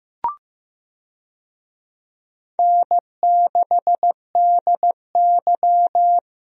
Morse code, heard as a keyed tone, about 15 wpm, 700 Hz. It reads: N6DY
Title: USA Call Signs - 4 Characters - Repeated 3x - 15wpm